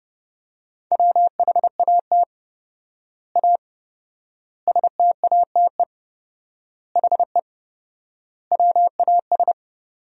Morse code, heard as a keyed tone, about 28 wpm, 700 Hz. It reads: WHAT A STATE HE WAS
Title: 4x Word-Spacing - Sentences from Top 400 Words - 30wpm